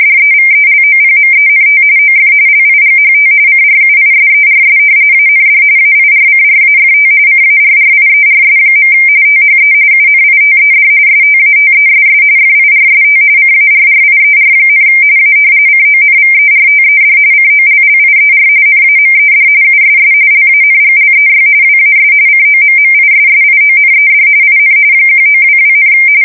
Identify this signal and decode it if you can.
RTTY, about 45.45 baud. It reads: DEVELOPMENT OF LIGHT CRUISERS AND AIRCRAFT CARRIERS. THE UNITED STATESS FIRST CARRIER, A CONVERTED COLLIER NAMED USS LANGLEY WAS COMMISSIONED IN 1922,
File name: fsk170 / usnavy4